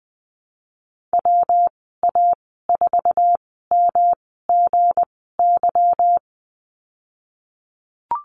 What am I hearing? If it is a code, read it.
Morse code, heard as a keyed tone, about 20 wpm, 700 Hz. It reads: W A 4 M G Y
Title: 200 Lesson 32g - Letter k - Call Signs 20wpm